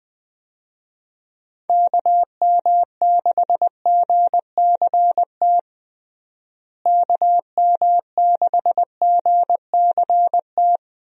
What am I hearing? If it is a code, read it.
Morse code, heard as a keyed tone, about 21 wpm, 700 Hz. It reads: KM6GCT KM6GCT
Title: USA Call Signs - 6 Characters - Repeated 2x - 20wpm